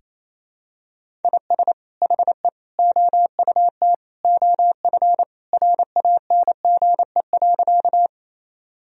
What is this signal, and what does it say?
Morse code, heard as a keyed tone, about 28 wpm, 700 Hz. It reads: IS HE OUT OF RANGE.
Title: Sentences from Top 1000 Words - Review Part 4 - 28wpm